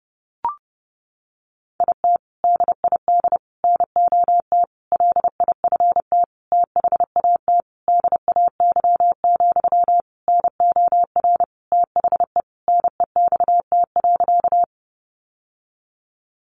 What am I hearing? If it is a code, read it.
Morse code, heard as a keyed tone, about 28 wpm, 700 Hz. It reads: IT DID NOT LIFT THAT DAY, NOR THE NEXT.
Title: Sentences from Top 1500 Words 30wpm